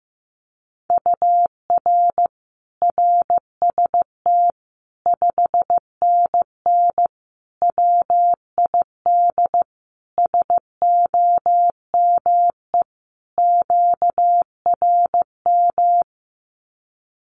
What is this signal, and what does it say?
Morse code, heard as a keyed tone, about 15 wpm, 700 Hz. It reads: UR RST 5NN WID SOME QRM
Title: Common QSO Phrases 15wpm